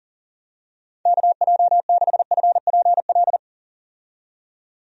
Morse code, heard as a keyed tone, about 40 wpm, 700 Hz. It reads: KJ6FPL
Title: USA Call Signs - 6 Characters - 40wpm